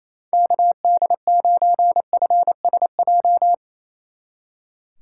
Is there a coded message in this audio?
Yes: Morse code from 0.3 to 3.5 s, heard as a keyed tone, about 36 wpm, 700 Hz. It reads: KD9FSJ